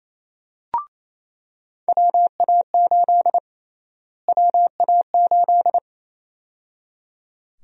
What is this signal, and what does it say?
Morse code, heard as a keyed tone, about 28 wpm, 700 Hz. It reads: WA8 WA8
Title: USA Call Sign Prefixes - Repeated 2x - 28wpm